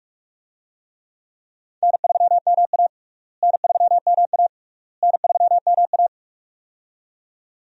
Morse code, heard as a keyed tone, about 41 wpm, 700 Hz. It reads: N3MA N3MA N3MA
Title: USA Call Signs - 4 Characters - Repeated 3x - 45wpm